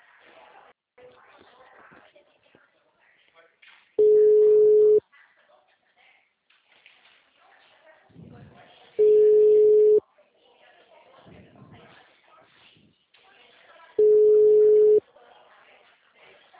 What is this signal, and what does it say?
Ringback tone (Europe, 425 Hz), with speech.
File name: out- -2020- - - .72771